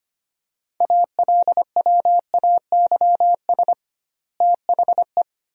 Morse code, heard as a keyed tone, about 25 wpm, 700 Hz. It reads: ALWAYS THE
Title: 2x Word-Spacing - Sentences from Top 300 Words - 25wpm